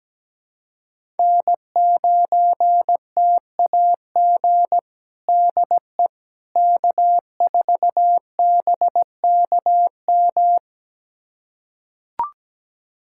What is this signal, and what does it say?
Morse code, heard as a keyed tone, about 17 wpm, 700 Hz. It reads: N9TAG DE K4BKM
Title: USA Callsign DE Callsign - 5 Characters - 17wpm